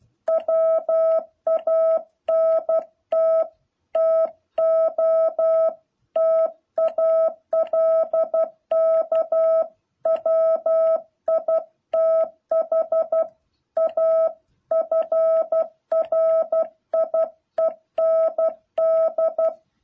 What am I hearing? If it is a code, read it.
Morse code, heard as a keyed tone, about 12 wpm, 660 Hz. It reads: WANT TO TALK WITH A FRIEND
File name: jh8cbh / 2021cw